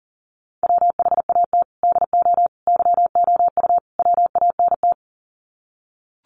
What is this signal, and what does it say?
Morse code, heard as a keyed tone, about 40 wpm, 700 Hz. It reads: WHAT DO YOU WANT
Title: Sets of 4 Words 40wpm